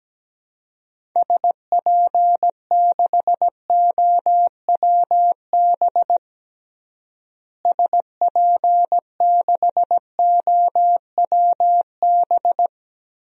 Morse code, heard as a keyed tone, about 17 wpm, 700 Hz. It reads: SP6OWB SP6OWB
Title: DXCC-100 Call Signs - 6 Characters - Repeated 2x - 17wpm